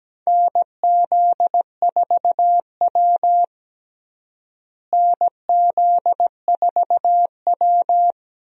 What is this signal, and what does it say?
Morse code, heard as a keyed tone, about 17 wpm, 700 Hz. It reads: NZ4W NZ4W
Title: USA Call Signs - 4 Characters - Repeated 3x - 17wpm